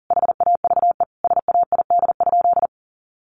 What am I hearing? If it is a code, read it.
Morse code, heard as a keyed tone, about 40 wpm, 700 Hz. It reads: HAVE SAID?
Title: Sentences from Top 1500 Words 40wpm